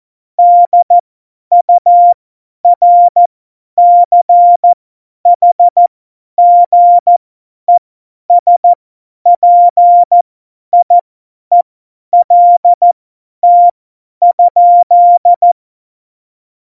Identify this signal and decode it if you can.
Morse code, heard as a keyed tone, about 14 wpm, spaced as for 10 wpm, 700 Hz. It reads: DURCHGESPIELT?
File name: Morsekurs / Witze / Witz 15. Januar 2019, 50 F70